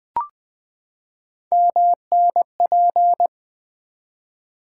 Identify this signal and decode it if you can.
Morse code, heard as a keyed tone, about 20 wpm, 700 Hz. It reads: MNP
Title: Three Letter-Number Combinations - 20wpm